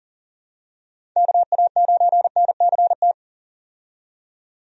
Morse code, heard as a keyed tone, about 40 wpm, 700 Hz. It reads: KA9NCT
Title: USA Call Signs - 6 Characters - 40wpm